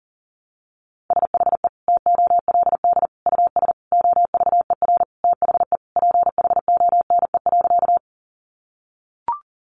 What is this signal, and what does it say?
Morse code, heard as a keyed tone, about 40 wpm, 700 Hz. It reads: SHE TOLD US OVER THE PHONE.
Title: Sentences from Top 1100 Words 40wpm